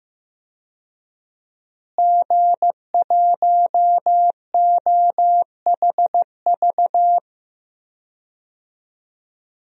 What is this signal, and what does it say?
Morse code, heard as a keyed tone, about 15 wpm, 700 Hz. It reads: G1OHV
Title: DXCC-100 Call Signs - 5 Characters - Repeated 2x - 15wpm